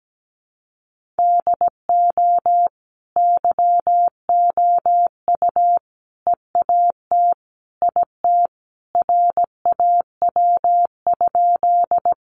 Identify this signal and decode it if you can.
Morse code, heard as a keyed tone, about 17 wpm, 700 Hz. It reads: DO YOU EAT IT RAW?
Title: Sentences from Top 1500 Words 17wpm